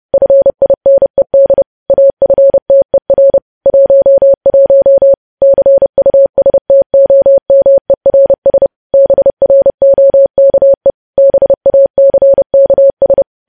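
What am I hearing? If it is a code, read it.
Morse code, heard as a keyed tone, about 30 wpm, 550 Hz. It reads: FINED AFTER 11 CUSTOMERS BROKE BACKS
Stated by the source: News headlines sent in Morse at 30 wpm, preceded by VVV.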